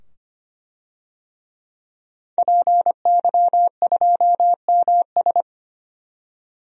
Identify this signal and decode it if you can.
Morse code, heard as a keyed tone, about 25 wpm, 700 Hz. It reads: PY2MS
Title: DXCC-100 Call Signs - 5 Characters - 25wpm